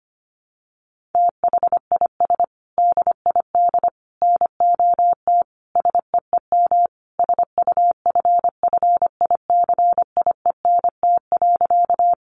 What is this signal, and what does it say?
Morse code, heard as a keyed tone, about 25 wpm, 700 Hz. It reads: THIS DID NOT SEEM SUFFICIENT.
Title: Sentences from Top 1400 Words 25wpm